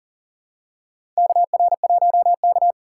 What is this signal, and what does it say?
Morse code, heard as a keyed tone, about 40 wpm, 700 Hz. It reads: KR1K
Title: USA Call Signs - 4 Characters - 2x Character Spacing - 40wpm